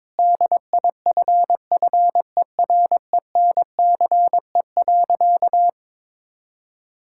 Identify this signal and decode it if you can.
Morse code, heard as a keyed tone, about 22 wpm, 700 Hz. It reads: DIFFERENCE.
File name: practice / 22wpm / Sentences with 3-Word Phrases - 22wpm